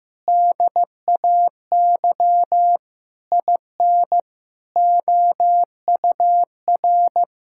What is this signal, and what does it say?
Morse code, heard as a keyed tone, about 15 wpm, 700 Hz. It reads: DAY IN OUR
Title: Sentences from Top 1600 Words 15wpm